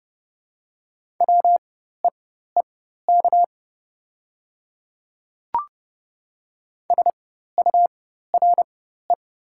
Morse code, heard as a keyed tone, about 30 wpm, spaced as for 14 wpm, 690 Hz. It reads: WEEK SURE
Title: 4x Character Spacing - Top 500 Words - 30wpm